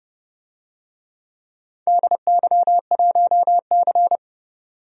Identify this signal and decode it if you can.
Morse code, heard as a keyed tone, about 28 wpm, 700 Hz. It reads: DY1C
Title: DXCC-100 Call Signs - 4 Characters - 30wpm